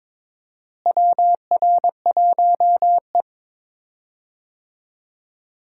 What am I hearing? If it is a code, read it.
Morse code, heard as a keyed tone, about 22 wpm, 700 Hz. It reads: WR1E
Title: USA Call Signs - 4 Characters - Repeated 3x - 22wpm